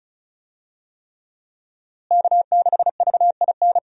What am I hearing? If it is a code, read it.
Morse code, heard as a keyed tone, about 36 wpm, 700 Hz. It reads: K6VIN
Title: USA Call Signs - 5 Characters - Repeated 3x - 35wpm